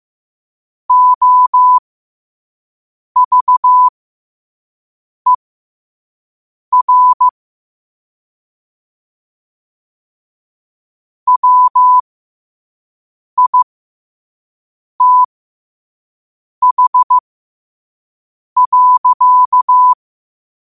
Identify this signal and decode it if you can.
Morse code, heard as a keyed tone, about 15 wpm, spaced as for 5 wpm, 1000 Hz. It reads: OVER WITH.
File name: qotdfiles / 5wpm